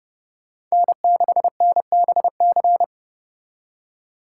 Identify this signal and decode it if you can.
Morse code, heard as a keyed tone, about 35 wpm, 700 Hz. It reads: N6NBC
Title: Speed Racing - USA Call Signs 30wpm